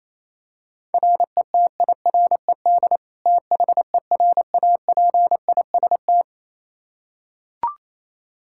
Morse code, heard as a keyed tone, about 28 wpm, 700 Hz. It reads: RETIRED THERAPIST
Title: Practice QSO - Exchange 3 - More Info - 28wpm